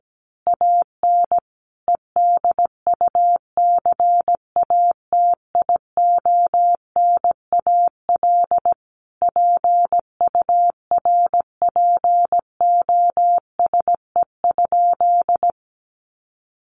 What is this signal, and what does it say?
Morse code, heard as a keyed tone, about 17 wpm, 700 Hz. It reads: AN EDUCATIONAL PURPOSE?
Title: Sentences from Top 1200 Words 17wpm